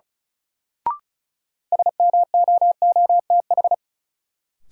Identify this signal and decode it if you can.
Morse code, heard as a keyed tone, about 36 wpm, 700 Hz. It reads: SMOOTH
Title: Sentences from Top 1400 Words - New Word Drill Alt - 35wpm